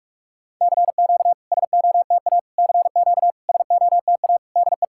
Morse code, heard as a keyed tone, about 46 wpm, 690 Hz. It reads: CQ SOTA CQ SOTA DE
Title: SOTA QSO Practice 45wpm